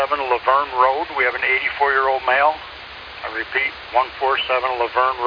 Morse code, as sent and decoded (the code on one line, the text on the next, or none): none